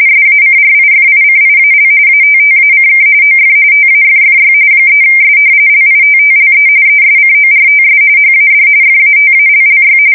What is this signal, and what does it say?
RTTY, about 45.45 baud. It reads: SACRIFICIAL ATTACK OF TAFFY 3S LITTLE ESCORTS HAD FATALLY
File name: fsk170 / battleshipnj